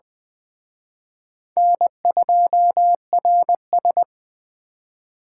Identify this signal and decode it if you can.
Morse code, heard as a keyed tone, about 20 wpm, 700 Hz. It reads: N2RS
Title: Speed Racing - USA 1x2 Call Signs 17wpm